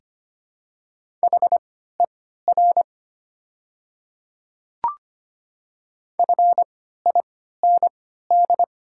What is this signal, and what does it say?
Morse code, heard as a keyed tone, about 25 wpm, spaced as for 14 wpm, 700 Hz. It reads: HER FIND
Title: 3x Character Spacing - Top 100 Words - 25wpm